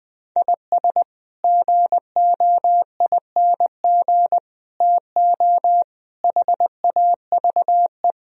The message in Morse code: IS GOING TO HAVE

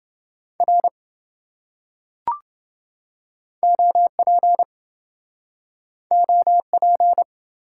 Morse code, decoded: R OP OP

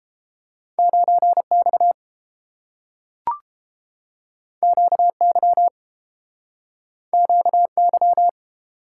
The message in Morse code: 9X QY QY

